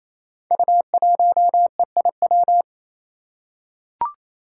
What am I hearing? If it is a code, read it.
Morse code, heard as a keyed tone, about 29 wpm, 700 Hz. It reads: U1EIW
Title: Five Challenging Letter-Number Combinations 28wpm